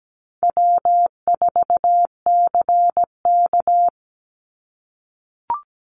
Morse code, heard as a keyed tone, about 17 wpm, 700 Hz. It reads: W4CK
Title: Easy Call Signs - 17wpm